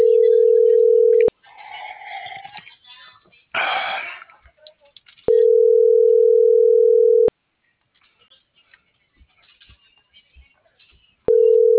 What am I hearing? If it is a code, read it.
Ringback tone (North America), 440 + 480 Hz.